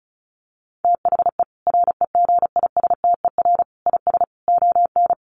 Morse code, heard as a keyed tone, about 34 wpm, 700 Hz. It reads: THE REGISTER IS ON
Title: Sentences from Top 1200 Words 35wpm